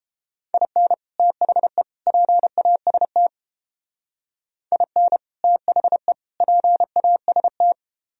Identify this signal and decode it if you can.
Morse code, heard as a keyed tone, about 31 wpm, 700 Hz. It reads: IN THE PAST IN THE PAST